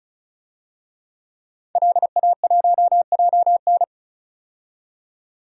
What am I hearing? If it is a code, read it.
Morse code, heard as a keyed tone, about 36 wpm, 700 Hz. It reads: LA1JN
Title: DXCC-100 Call Signs - 5 Characters - 35wpm